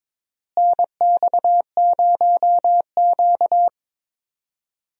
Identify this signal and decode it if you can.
Morse code, heard as a keyed tone, about 22 wpm, 700 Hz. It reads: NX0Q
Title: USA Call Signs - 4 Characters - Repeated 3x - 22wpm